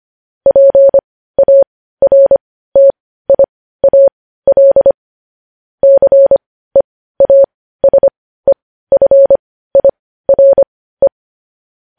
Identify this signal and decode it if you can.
Morse code, heard as a keyed tone, about 25 wpm, spaced as for 15 wpm, 550 Hz. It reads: PARTIAL CEASEFIRE
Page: News headlines in Morse code at 15 wpm